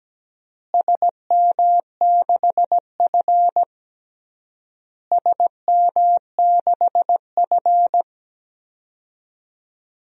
Morse code, heard as a keyed tone, about 17 wpm, 700 Hz. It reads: SM6F SM6F
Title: DXCC-100 Call Signs - 4 Characters - Repeated 3x - 17wpm